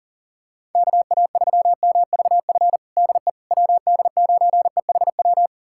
Morse code, heard as a keyed tone, about 40 wpm, 700 Hz. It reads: KA3MVF DE WD9EHW
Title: USA Callsign DE Callsign 40wpm